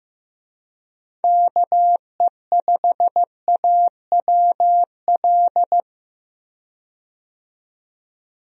Morse code, heard as a keyed tone, about 15 wpm, 700 Hz. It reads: KE5AWL
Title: USA Call Signs - 6 Characters - 15wpm